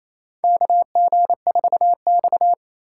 Morse code, heard as a keyed tone, about 28 wpm, 700 Hz. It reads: KG4X
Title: USA Call Signs - 4 Characters - 4x Character Spacing - 28wpm